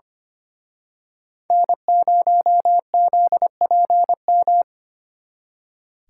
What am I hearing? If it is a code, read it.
Morse code, heard as a keyed tone, about 29 wpm, 700 Hz. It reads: N0ZPM
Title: Speed Racing - USA 1x3 Call Signs 25wpm